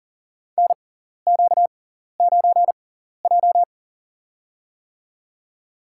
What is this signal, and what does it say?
Morse code, heard as a keyed tone, about 37 wpm, spaced as for 14 wpm, 700 Hz. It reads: NQ9J
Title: USA Call Signs - 4 Characters - 6x Character Spacing - 40wpm